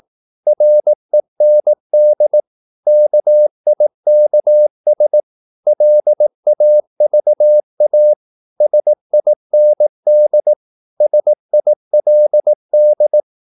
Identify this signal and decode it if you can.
Morse code, heard as a keyed tone, about 18 wpm, 600 Hz. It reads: REND KIKS LAVA SIND SILD
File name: Korte ord DK 18wpm